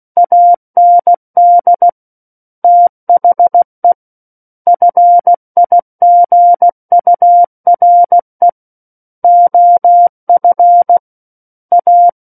Morse code, heard as a keyed tone, about 16 wpm, 700 Hz. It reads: AND THE FIGURE OF A
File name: Book One / 07-Chapter 7 - 16 WPM